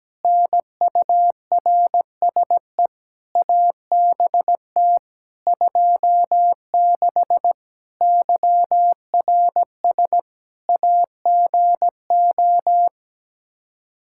Morse code, heard as a keyed tone, about 17 wpm, 700 Hz. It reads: NURSE ABT 26 YRS AGO
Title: Practice QSO - Exchange 3 - More Info - 17wpm